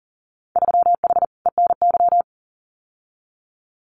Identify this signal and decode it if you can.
Morse code, heard as a keyed tone, about 40 wpm, 700 Hz. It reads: 3H ENY
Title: Winter Field Day Exchanges Encore 40wpm